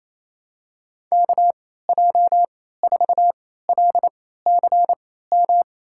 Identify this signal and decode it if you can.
Morse code, heard as a keyed tone, about 28 wpm, spaced as for 16 wpm, 700 Hz. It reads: KJ4LCM
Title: USA Call Signs - 6 Characters - 3x Character Spacing - 28wpm